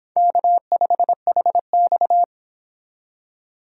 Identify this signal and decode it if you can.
Morse code, heard as a keyed tone, about 25 wpm, 700 Hz. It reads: K5HX